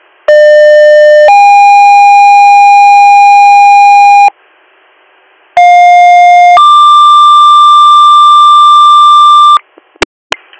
Two-tone page, heard as a one-second tone, then a three-second tone, about 600 then 810 Hz, about 710 then 1180 Hz. It is heard from 0.3 to 9.6 s.